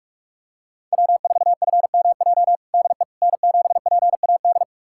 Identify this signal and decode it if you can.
Morse code, heard as a keyed tone, about 45 wpm, 700 Hz. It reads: W4FMJ DE N7PAD
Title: USA Callsign DE Callsign - 5 Characters - 45wpm